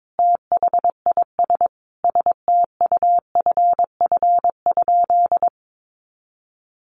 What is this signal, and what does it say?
Morse code, heard as a keyed tone, about 22 wpm, 700 Hz. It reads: THIS STUFF?
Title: Sentences from Top 1400 Words 22wpm